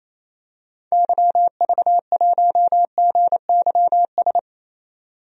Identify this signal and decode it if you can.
Morse code, heard as a keyed tone, about 28 wpm, 700 Hz. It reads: YV1GYS
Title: DXCC-100 Call Signs - 6 Characters - 28wpm